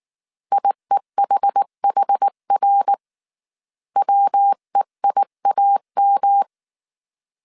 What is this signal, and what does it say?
Morse code, heard as a keyed tone, about 19 wpm, 800 Hz. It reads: IEHHR WEIAM
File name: CW / DET / 1 3 1500C S B 18wpm 1916